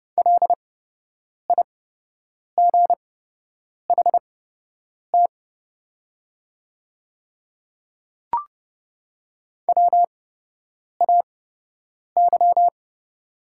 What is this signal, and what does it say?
Morse code, heard as a keyed tone, about 30 wpm, spaced as for 8 wpm, 700 Hz. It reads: LIGHT WAY
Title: 8x Character Spacing - Top 300 Words - 30wpm